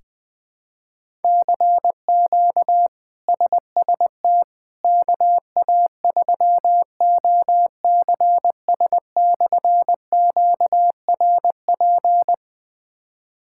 Morse code, heard as a keyed tone, about 21 wpm, 700 Hz. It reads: CQ SST KA3OCS/QRP
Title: K1USN SST QSO Practice - 20wpm